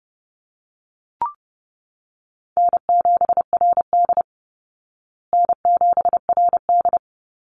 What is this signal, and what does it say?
Morse code, heard as a keyed tone, about 30 wpm, 700 Hz. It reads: N7RD N7RD
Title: Call Signs Repeated 3x - 30wpm